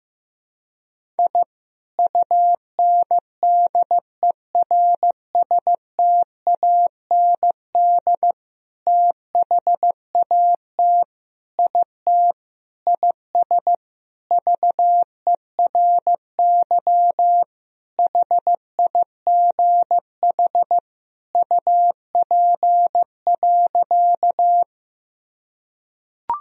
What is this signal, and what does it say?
Morse code, heard as a keyed tone, about 15 wpm, 700 Hz. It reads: I UNDERSTAND THAT IT IS VERY HIGH UP.
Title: Sentences from Top 1000 Words - Review Part 1 - 15wpm